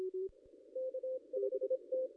Morse code, heard as a keyed tone, about 26 wpm, 520 Hz. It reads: K5T
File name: contests / ss-cw-k5tr-op-n5rz -0500